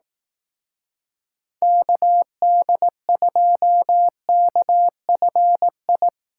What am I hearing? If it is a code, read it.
Morse code, heard as a keyed tone, about 18 wpm, 700 Hz. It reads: KD2KFI
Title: Speed Racing - USA Call Signs 15wpm